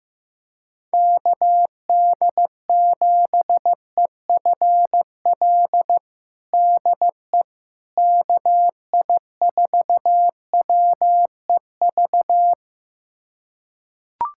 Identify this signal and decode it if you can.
Morse code, heard as a keyed tone, about 15 wpm, 700 Hz. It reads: KD7EFL DE KI4WEV
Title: USA Callsign DE Callsign - 6 Characters - 15wpm